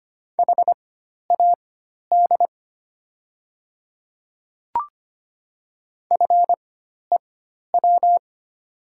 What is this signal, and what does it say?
Morse code, heard as a keyed tone, about 25 wpm, spaced as for 12 wpm, 700 Hz. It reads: HAD FEW
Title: 4x Character Spacing - Top 300 Words - 25wpm